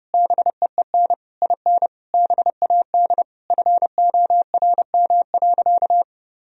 Morse code, heard as a keyed tone, about 30 wpm, 700 Hz. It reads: BEEN IN BAD FORM.
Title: Sentences from Top 1000 Words - Review Part 1 - 30wpm